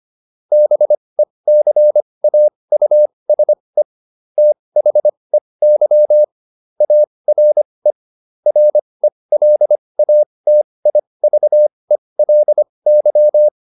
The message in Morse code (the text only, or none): BECAUSE THEY ARE RELATIVELY